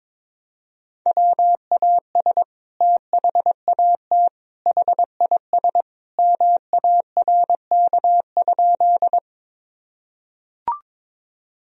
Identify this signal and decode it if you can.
Morse code, heard as a keyed tone, about 22 wpm, 700 Hz. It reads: WAS THAT HIS MARK?
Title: Sentences from Top 1000 Words - Review Part 1 - 22wpm